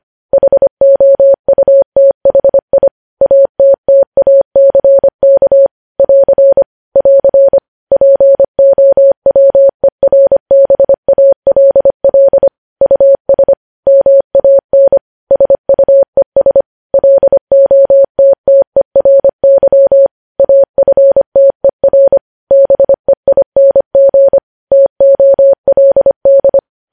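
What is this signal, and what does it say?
Morse code, heard as a keyed tone, about 25 wpm, 550 Hz. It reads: HOUTHI ATTACK <AR> <AR> POWERBALL US MAN SUES LOTTERY AFTER BEING TOLD
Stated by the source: Morse news headlines sent at 25 wpm.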